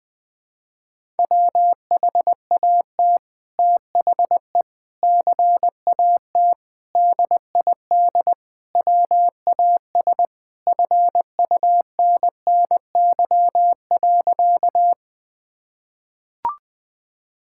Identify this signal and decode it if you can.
Morse code, heard as a keyed tone, about 21 wpm, 700 Hz. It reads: WHAT THE CAT DID WAS FUNNY.
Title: Sentences with 2-Word Phrases - 20wpm